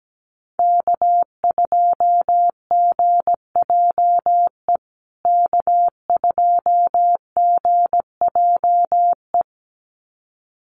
Morse code, heard as a keyed tone, about 17 wpm, 700 Hz. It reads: K2GJE K2GJE
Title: USA Call Signs Repeated 2x 17wpm